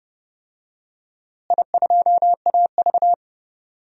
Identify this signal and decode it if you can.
Morse code, heard as a keyed tone, about 30 wpm, 700 Hz. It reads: I2AV
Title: DXCC-100 Call Signs - 4 Characters - Repeated 2x - 30wpm